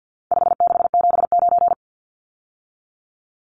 Morse code, heard as a keyed tone, about 50 wpm, 700 Hz. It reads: 5679
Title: Numbers - 4-Digit 50wpm